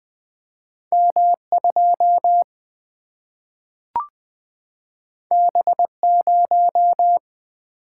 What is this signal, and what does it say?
Morse code, heard as a keyed tone, about 20 wpm, 700 Hz. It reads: M 2 B 0
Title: Single-Letter Single-Number 20wpm